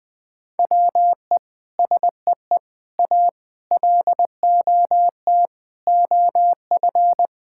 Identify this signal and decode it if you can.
Morse code, heard as a keyed tone, about 21 wpm, 700 Hz. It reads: WE SEE A LOT OF
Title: Sentences with 9-Word Phrases - 20wpm